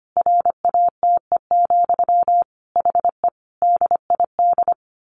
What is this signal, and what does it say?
Morse code, heard as a keyed tone, about 25 wpm, 700 Hz. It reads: RATE, HE DID
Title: Sentences from Top 700 Words 25wpm